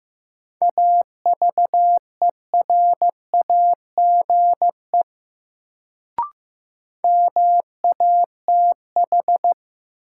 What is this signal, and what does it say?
Morse code, heard as a keyed tone, about 15 wpm, 700 Hz. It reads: AVERAGE MATH
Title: Sentences from Top 1000 Words - Review Part 6 - New Word Drill - 15wpm